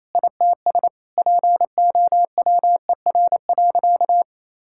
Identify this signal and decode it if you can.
Morse code, heard as a keyed tone, about 28 wpm, 700 Hz. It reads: ITS POWER.
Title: Sentences from Top 1000 Words - Review Part 5 - 28wpm